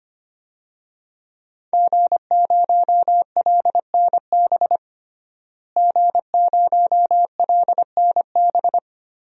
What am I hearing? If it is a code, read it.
Morse code, heard as a keyed tone, about 25 wpm, 700 Hz. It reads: G0LNB G0LNB
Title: DXCC-100 Call Signs - 5 Characters - Repeated 3x - 25wpm